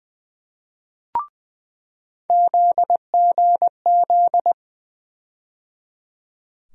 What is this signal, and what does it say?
Morse code, heard as a keyed tone, about 21 wpm, 700 Hz. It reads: ZGZ
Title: USA Call Sign Postfixes - 3 Letters 20wpm